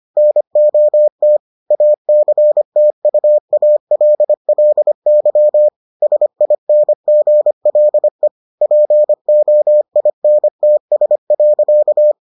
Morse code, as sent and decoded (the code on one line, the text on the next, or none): -. --- - / .- -.-. - ..- .- .-.. .-.. -.-- / ... .. -. --. .-.. / .--. --- .. -. - ... .-.-.-
NOT ACTUALLY SINGLE POINTS.